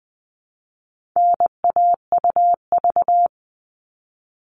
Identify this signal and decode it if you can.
Morse code, heard as a keyed tone, about 20 wpm, 700 Hz. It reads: NAUV
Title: All Cut Numbers - 4-Digit 20wpm